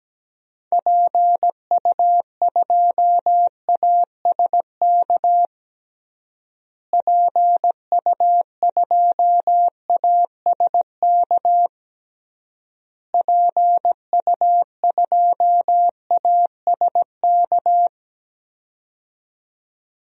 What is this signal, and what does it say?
Morse code, heard as a keyed tone, about 17 wpm, 700 Hz. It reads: PU2ASK PU2ASK PU2ASK
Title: DXCC-100 Call Signs - 6 Characters - Repeated 3x - 17wpm